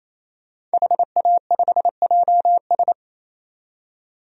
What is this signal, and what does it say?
Morse code, heard as a keyed tone, about 28 wpm, 700 Hz. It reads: HA5JS
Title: DXCC-100 Call Signs - 5 Characters - 28wpm